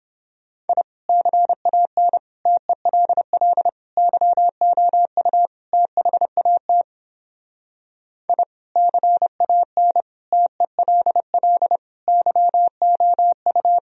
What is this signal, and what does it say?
Morse code, heard as a keyed tone, about 29 wpm, 700 Hz. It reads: I CAN TELL YOU THAT I CAN TELL YOU